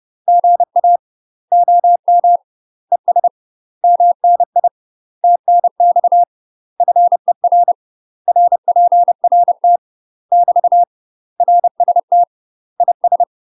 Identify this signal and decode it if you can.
Morse code, heard as a keyed tone, about 30 wpm, 700 Hz. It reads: GA OM ES MNI TNX FER RPRT <BT> RST IS
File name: UK MASTER QSOs TRAINING / 30WPM / UK MASTER QSOs 0049